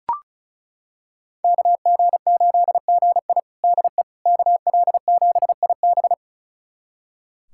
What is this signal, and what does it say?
Morse code, heard as a keyed tone, about 36 wpm, 700 Hz. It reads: KG8GI DE KL7IB
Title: USA Callsign DE Callsign 35wpm